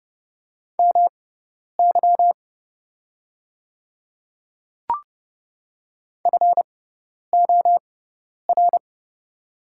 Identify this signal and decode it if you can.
Morse code, heard as a keyed tone, about 30 wpm, spaced as for 10 wpm, 690 Hz. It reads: MY FOR
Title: 6x Character Spacing - Top 100 Words - 30wpm